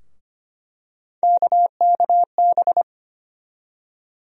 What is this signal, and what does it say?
Morse code, heard as a keyed tone, about 25 wpm, 700 Hz. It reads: KKB
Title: USA Call Sign Postfixes - 3 Letters 25wpm